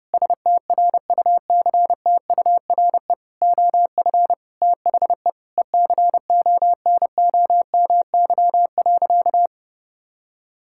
Morse code, heard as a keyed tone, about 32 wpm, 700 Hz. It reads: STRUCTURE OF THE ECONOMY.
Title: Sentences from Top 1000 Words - Review Part 6 - 30wpm